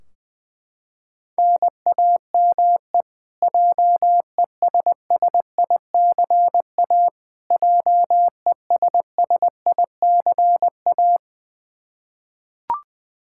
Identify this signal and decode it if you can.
Morse code, heard as a keyed tone, about 20 wpm, 700 Hz. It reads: NAME JESSICA JESSICA
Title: Practice QSO - Exchange 1 - Beginning - 20wpm